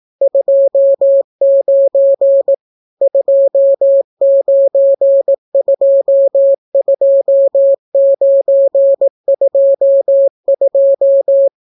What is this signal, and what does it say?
Morse code, heard as a keyed tone, about 18 wpm, 550 Hz. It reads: 29 2922922